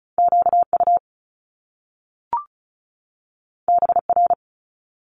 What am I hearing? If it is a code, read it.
Morse code, heard as a keyed tone, about 36 wpm, 700 Hz. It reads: QU BR
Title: Top Two Letter Combinations in Words 35wpm